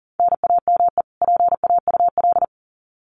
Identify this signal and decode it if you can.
Morse code, heard as a keyed tone, about 40 wpm, 700 Hz. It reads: NAME PAUL
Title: Common QSO Phrases 40wpm